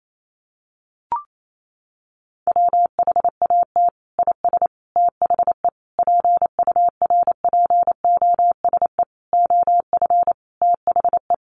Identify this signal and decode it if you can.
Morse code, heard as a keyed tone, about 28 wpm, 700 Hz. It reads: WHAT IS THE PURPOSE OF THE
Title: Sentences from Top 1400 Words 28wpm